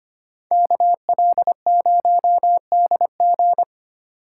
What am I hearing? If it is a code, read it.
Morse code, heard as a keyed tone, about 29 wpm, 700 Hz. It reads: KL0DG
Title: Speed Racing - USA 2x2 Call Signs 25wpm